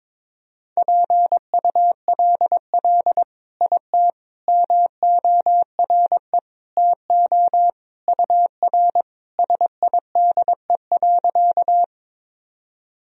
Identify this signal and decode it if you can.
Morse code, heard as a keyed tone, about 22 wpm, 700 Hz. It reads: PULL IT MORE TO UR SIDE.
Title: Sentences from Top 400 Words and Abbreviations 22wpm